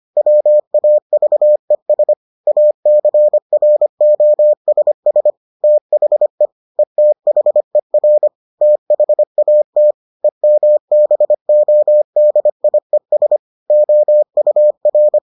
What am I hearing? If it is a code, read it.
Morse code, heard as a keyed tone, about 25 wpm, 600 Hz. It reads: WAVES ACROSS THE ETHER THAT EMBODIES OUR